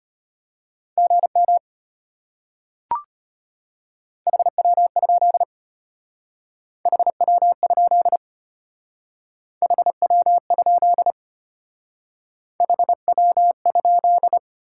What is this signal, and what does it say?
Morse code, heard as a keyed tone, about 31 wpm, 700 Hz. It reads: GM HW? HW? HW? HW?
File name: practice / 25wpm / Speed-Racing Top 20 QSO Elements - 25wpm